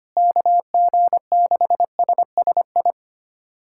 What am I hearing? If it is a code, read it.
Morse code, heard as a keyed tone, about 25 wpm, 700 Hz. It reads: KG6SSI
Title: USA Callsign DE Callsign - 6 Characters - 25wpm